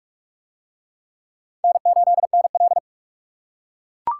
Morse code, heard as a keyed tone, about 42 wpm, 700 Hz. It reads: N8NL